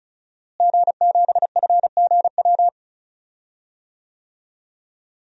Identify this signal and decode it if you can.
Morse code, heard as a keyed tone, about 34 wpm, 700 Hz. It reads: G7FGW
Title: DXCC-100 Call Signs - 5 Characters - Repeated 2x - 35wpm